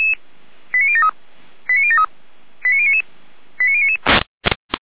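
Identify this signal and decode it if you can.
Five-tone selective call (ZVEI), heard as quick runs of five tones, tones of about 70 ms. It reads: 89073 89073 89099 89099
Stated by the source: ZVEI five-tone call 89220 received by SDR, archived by Feuerwehr Orlamünde fire brigade.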